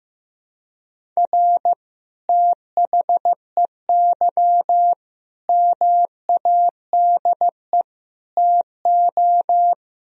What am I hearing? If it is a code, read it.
Morse code, heard as a keyed tone, about 15 wpm, 700 Hz. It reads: R THEY MADE TO
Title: Sentences from Top 400 Words and Abbreviations 15wpm